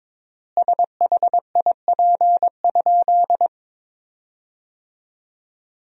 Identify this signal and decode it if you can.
Morse code, heard as a keyed tone, about 22 wpm, 700 Hz. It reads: SHIP?
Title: 3x Word-Spacing - Sentences from Top 400 Words - 22wpm